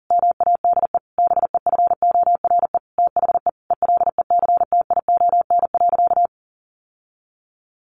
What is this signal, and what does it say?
Morse code, heard as a keyed tone, about 40 wpm, 700 Hz. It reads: MADE BEFORE THE ELECTION.
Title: Sentences from Top 1500 Words 40wpm